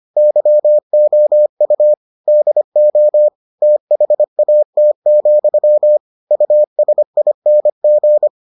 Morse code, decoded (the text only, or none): YOU DO THAT, USING